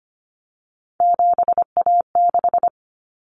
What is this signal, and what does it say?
Morse code, heard as a keyed tone, about 25 wpm, 700 Hz. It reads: 7A6
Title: Common Cut Numbers - 3-Digit 25wpm